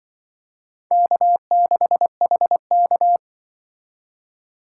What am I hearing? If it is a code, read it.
Morse code, heard as a keyed tone, about 25 wpm, 700 Hz. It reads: K6HK
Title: Speed Racing - USA Call Signs 20wpm